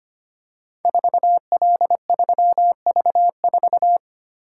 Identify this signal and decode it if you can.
Morse code, heard as a keyed tone, about 25 wpm, 700 Hz. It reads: 4L3V4
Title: Code Groups - 5 Letter:Numbers - 25wpm